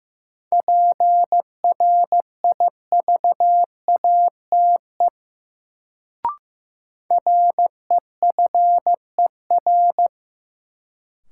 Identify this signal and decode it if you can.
Morse code, heard as a keyed tone, about 15 wpm, 700 Hz. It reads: PRIVATE REFER
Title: Sentences from Top 1000 Words - New Word Drill - 15wpm